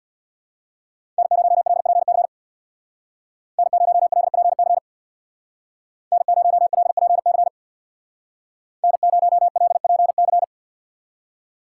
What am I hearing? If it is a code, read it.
Morse code, heard as a keyed tone, about 65 wpm, 700 Hz. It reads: N0LPC N0LPC N0LPC N0LPC